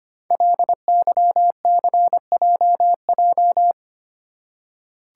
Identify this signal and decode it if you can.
Morse code, heard as a keyed tone, about 25 wpm, 700 Hz. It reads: LYCJJ
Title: Code Groups - 5 Letter:Numbers - 25wpm